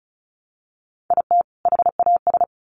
Morse code, heard as a keyed tone, about 34 wpm, 700 Hz. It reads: IT HAS
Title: Sets of 2 Words 35wpm